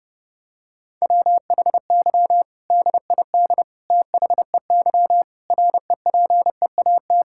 Morse code, heard as a keyed tone, about 32 wpm, 700 Hz. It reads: WHY DID THEY REPEAT
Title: Sentences from Top 1600 Words 30wpm